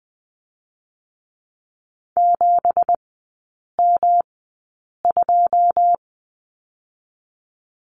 Morse code, heard as a keyed tone, about 20 wpm, spaced as for 8 wpm, 700 Hz. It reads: 7M2
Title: 184 Lesson 30e - Letter b - Three Characters 20wpm